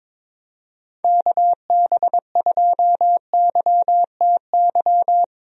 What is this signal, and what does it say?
Morse code, heard as a keyed tone, about 22 wpm, 700 Hz. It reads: KB2YTY
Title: USA Call Signs - 6 Characters - Repeated 3x - 22wpm